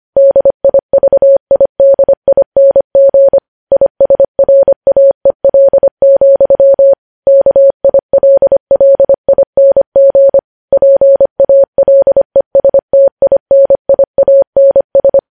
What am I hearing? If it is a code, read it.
Morse code, heard as a keyed tone, about 25 wpm, 550 Hz. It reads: DIVIDING ISRAEL, KILLING PALESTINIANS
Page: news headlines in morse code at 25 wpm